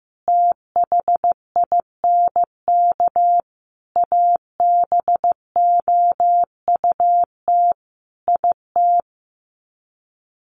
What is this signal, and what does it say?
Morse code, heard as a keyed tone, about 15 wpm, 700 Hz. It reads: THINK ABOUT IT
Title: Sets of 5 Words Encore2 15wpm